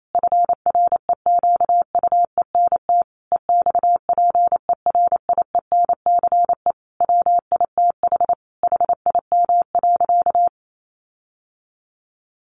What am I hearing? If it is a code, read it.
Morse code, heard as a keyed tone, about 28 wpm, 700 Hz. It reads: FREQUENT EXPERIENCE WITH HIM.
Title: Sentences from Top 1400 Words 28wpm